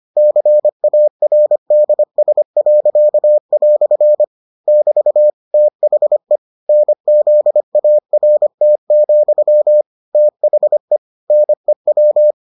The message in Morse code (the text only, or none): CARDS." <BT> THE NZART, THE NEW